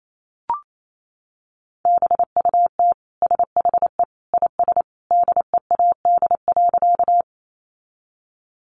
Morse code, heard as a keyed tone, about 28 wpm, 700 Hz. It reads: BUT SHE IS DEAD.
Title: Sentences from Top 1500 Words 28wpm